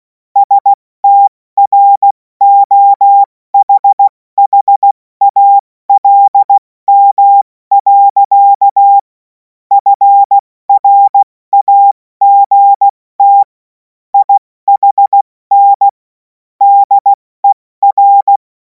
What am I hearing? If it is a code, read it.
Morse code, heard as a keyed tone, about 16 wpm, 800 Hz. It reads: STROHHALM. FRAGT IHN DER
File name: Morsekurs / Witze / Witz 17. März 2026 70 F80